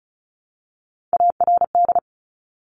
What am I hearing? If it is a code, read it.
Morse code, heard as a keyed tone, about 36 wpm, 700 Hz. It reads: ARD
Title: Top Three Letter Combinations in Words 35wpm